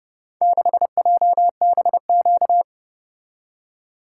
Morse code, heard as a keyed tone, about 30 wpm, 700 Hz. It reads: 6JBQ